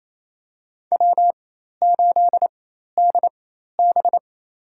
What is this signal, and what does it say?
Morse code, heard as a keyed tone, about 27 wpm, spaced as for 13 wpm, 700 Hz. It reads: W8DB
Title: USA Call Signs - 4 Characters - 4x Character Spacing - 28wpm